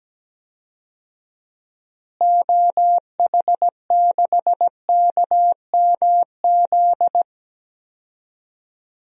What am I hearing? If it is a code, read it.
Morse code, heard as a keyed tone, about 17 wpm, 700 Hz. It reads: OH6KMZ
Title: DXCC-100 Call Signs - 6 Characters - Repeated 3x - 17wpm